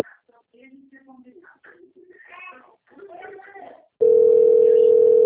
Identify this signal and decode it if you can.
Ringback tone (North America), 440 + 480 Hz, with speech.
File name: rg-1600-unknown- - - .50776